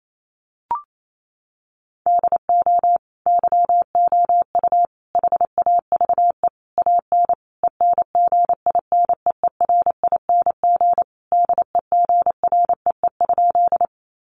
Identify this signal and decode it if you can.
Morse code, heard as a keyed tone, about 28 wpm, 700 Hz. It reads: DO YOU HAVE AN ENGINEERING DEGREE?
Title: Sentences from Top 1500 Words 28wpm